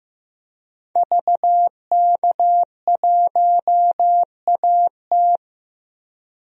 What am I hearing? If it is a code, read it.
Morse code, heard as a keyed tone, about 15 wpm, 700 Hz. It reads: VK1AT
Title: DXCC-100 Call Signs - 5 Characters - 15wpm